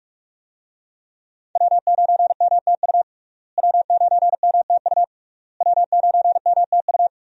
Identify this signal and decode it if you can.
Morse code, heard as a keyed tone, about 44 wpm, 700 Hz. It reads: W9MTU W9MTU W9MTU
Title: USA Call Signs - 5 Characters - Repeated 3x - 45wpm